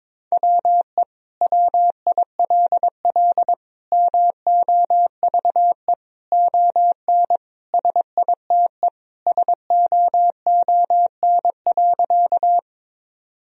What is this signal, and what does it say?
Morse code, heard as a keyed tone, about 22 wpm, 700 Hz. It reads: WE WILL MOVE ON SITE SOON.
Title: Sentences from Top 1000 Words - Review Part 4 - 22wpm